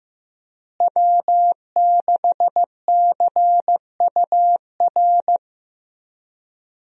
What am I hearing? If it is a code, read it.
Morse code, heard as a keyed tone, about 16 wpm, 700 Hz. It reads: W6CUR
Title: USA Call Signs - 5 Characters - Repeated 2x - 15wpm